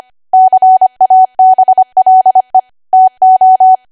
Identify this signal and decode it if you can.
Morse code, heard as a keyed tone, about 25 wpm, 750 Hz. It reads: CABLE TO
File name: Archive / 25 WPM / 25WPM